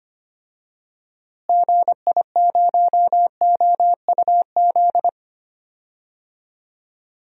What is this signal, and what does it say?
Morse code, heard as a keyed tone, about 25 wpm, 700 Hz. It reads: GI0OUZ
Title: DXCC-100 Call Signs - 6 Characters - Repeated 3x - 25wpm